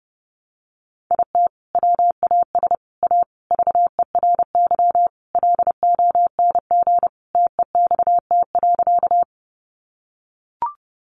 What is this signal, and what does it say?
Morse code, heard as a keyed tone, about 30 wpm, 700 Hz. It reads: IT WAS A VERY LONG TEXT.
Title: Sentences from Top 1200 Words 30wpm